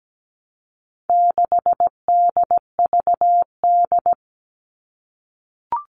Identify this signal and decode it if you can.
Morse code, heard as a keyed tone, about 17 wpm, 700 Hz. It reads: 6DVD
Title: All Cut Numbers - 4-Digit 17wpm